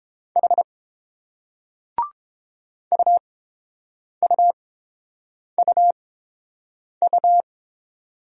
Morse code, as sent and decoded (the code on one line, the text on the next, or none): .... / ..- ..- ..- ..-
H UUUU